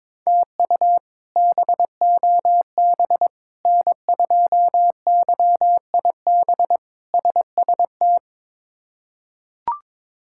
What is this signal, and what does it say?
Morse code, heard as a keyed tone, about 22 wpm, 700 Hz. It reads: TU BOB N2YIB SST
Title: K1USN SST QSO Practice - 22wpm